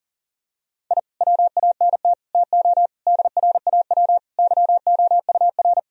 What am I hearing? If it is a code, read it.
Morse code, heard as a keyed tone, about 40 wpm, 700 Hz. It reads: I WANT TO DRAW YOUR
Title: Sentences from Top 1000 Words - Review Part 1 - 40wpm